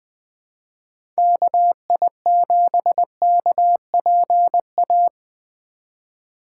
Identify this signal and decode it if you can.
Morse code, heard as a keyed tone, about 20 wpm, 700 Hz. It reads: KI7KPA
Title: USA Call Signs - 6 Characters - 2x Character Spacing - 20wpm